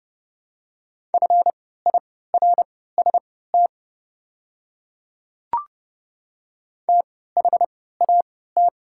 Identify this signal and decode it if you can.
Morse code, heard as a keyed tone, about 30 wpm, spaced as for 17 wpm, 700 Hz. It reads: FIRST THAT
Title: 3x Character Spacing - Top 200 Words - 30wpm